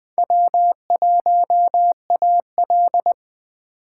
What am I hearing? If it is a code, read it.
Morse code, heard as a keyed tone, about 20 wpm, 700 Hz. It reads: W1AL